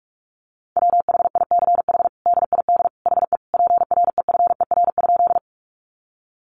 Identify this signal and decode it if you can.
Morse code, heard as a keyed tone, about 44 wpm, 700 Hz. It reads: WHICH DID HE PREFER?
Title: Sentences from Top 1100 Words 45wpm